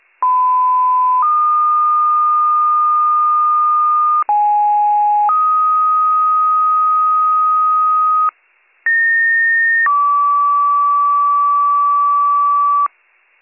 Two-tone page, heard as a one-second tone, then a three-second tone, about 1010 then 1220 Hz, about 830 then 1250 Hz, about 1850 then 1150 Hz. It is heard from 0.2 to 12.9 s.